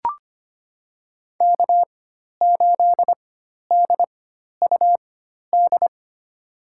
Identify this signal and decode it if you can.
Morse code, heard as a keyed tone, about 25 wpm, spaced as for 12 wpm, 700 Hz. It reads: K8DUD
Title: USA Call Signs - 5 Characters - 4x Character Spacing - 25wpm